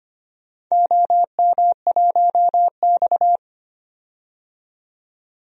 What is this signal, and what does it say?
Morse code, heard as a keyed tone, about 25 wpm, 700 Hz. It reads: OM1X
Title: DXCC-100 Call Signs - 4 Characters - 25wpm